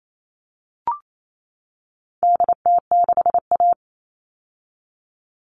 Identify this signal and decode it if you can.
Morse code, heard as a keyed tone, about 28 wpm, 700 Hz. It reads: DT6A
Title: All Cut Numbers - 4-Digit 28wpm